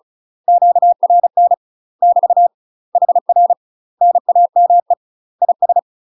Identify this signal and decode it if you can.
Morse code, heard as a keyed tone, about 36 wpm, 700 Hz. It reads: QRN <BT> HR NAME IS
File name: UK MASTER QSOs TRAINING / 35WPM / UK MASTER QSOs 0038